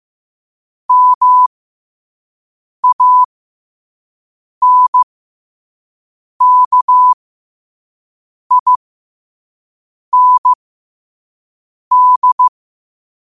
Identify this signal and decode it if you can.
Morse code, heard as a keyed tone, about 15 wpm, spaced as for 5 wpm, 1000 Hz. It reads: MANKIND
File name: qotdfiles / 5wpm